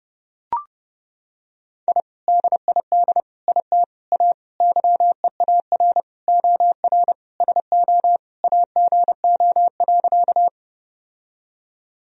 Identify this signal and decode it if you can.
Morse code, heard as a keyed tone, about 32 wpm, 700 Hz. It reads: I DID IT A YEAR OR SO AGO.
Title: Sentences from Top 1000 Words - Review Part 3 - 30wpm